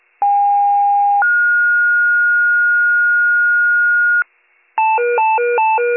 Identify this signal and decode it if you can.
Two-tone page, heard as a one-second tone, then a three-second tone, about 810 then 1430 Hz, with noise. It is heard from 0.2 to 4.2 s.